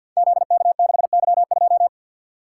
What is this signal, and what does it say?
Morse code, heard as a keyed tone, about 53 wpm, 700 Hz. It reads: ZK6Y2
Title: Code Groups - 5 Letter:Numbers - 50wpm